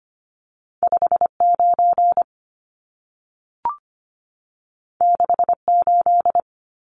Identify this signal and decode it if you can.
Morse code, heard as a keyed tone, about 25 wpm, 700 Hz. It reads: 59 68
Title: Numbers - 2-Digit 25wpm